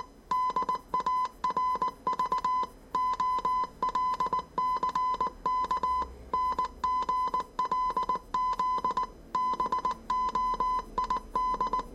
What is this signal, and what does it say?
Morse code, heard as a keyed tone, about 19 wpm, 1010 Hz. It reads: DARV OLCK NGLZ BOID
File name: 业余无线电 / ●未分类 / 字码80